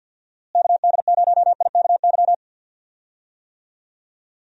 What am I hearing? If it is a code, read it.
Morse code, heard as a keyed tone, about 51 wpm, 700 Hz. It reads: KD0IKY
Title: USA Call Signs - 6 Characters - Repeated 3x - 50wpm